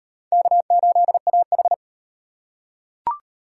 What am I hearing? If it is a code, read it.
Morse code, heard as a keyed tone, about 35 wpm, 700 Hz. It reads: K8AH